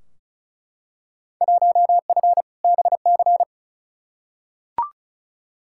Morse code, heard as a keyed tone, about 36 wpm, 700 Hz. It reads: 1F BC
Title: Field Day Exchanges Encore - 35wpm